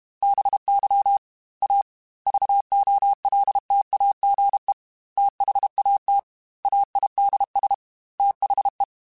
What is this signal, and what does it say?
Morse code, heard as a keyed tone, about 32 wpm, 800 Hz. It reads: BY A VOLTAGE THAT AIDS THE
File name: Technical Subjects / Technical Subjects-mp3s / XL @30